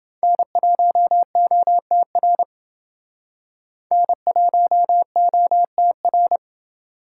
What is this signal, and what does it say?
Morse code, heard as a keyed tone, about 29 wpm, 700 Hz. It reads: N1OTR N1OTR